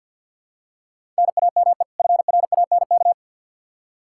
Morse code, heard as a keyed tone, about 50 wpm, 700 Hz. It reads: NAME FRANK
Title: Practice QSO - Exchange 1 - Beginning - 50wpm